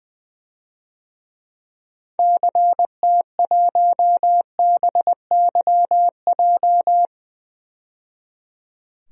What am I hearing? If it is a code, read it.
Morse code, heard as a keyed tone, about 20 wpm, 700 Hz. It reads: CT1BYJ
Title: DXCC-100 Call Signs - 6 Characters - 20wpm